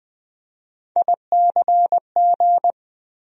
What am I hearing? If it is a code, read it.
Morse code, heard as a keyed tone, about 20 wpm, 700 Hz. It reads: ICG
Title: USA Call Sign Postfixes - 3 Letters 20wpm